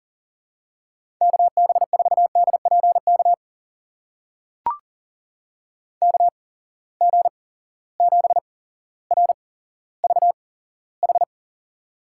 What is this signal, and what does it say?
Morse code, heard as a keyed tone, about 40 wpm, spaced as for 17 wpm, 700 Hz. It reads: KB4DPK KG7RVH
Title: USA Call Signs - 6 Characters - 8x Character Spacing - 40wpm